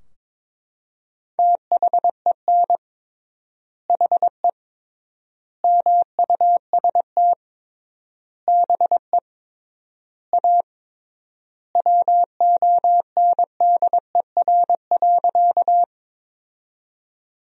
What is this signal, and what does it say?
Morse code, heard as a keyed tone, about 22 wpm, 700 Hz. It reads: THEN HE MUST BE A WONDER.
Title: 3x Word-Spacing - Sentences from Top 500 Words - 22wpm